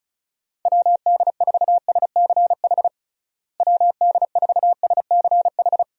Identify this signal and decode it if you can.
Morse code, heard as a keyed tone, about 36 wpm, 700 Hz. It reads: WD4SCH WD4SCH
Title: USA Call Signs - 6 Characters - Repeated 2x - 35wpm